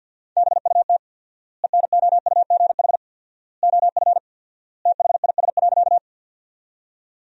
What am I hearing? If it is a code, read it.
Morse code, heard as a keyed tone, about 50 wpm, 700 Hz. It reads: BUT ENOUGH OF THIS.
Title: 4x Word-Spacing - Sentences from Top 400 Words - 50wpm